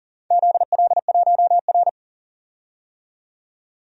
Morse code, heard as a keyed tone, about 40 wpm, 700 Hz. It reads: ZL1R
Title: DXCC-100 Call Signs - 4 Characters - 40wpm